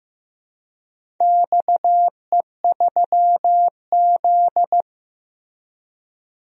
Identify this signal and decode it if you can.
Morse code, heard as a keyed tone, about 16 wpm, 700 Hz. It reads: XE3Z
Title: DXCC-100 Call Signs - 4 Characters - Repeated 2x - 15wpm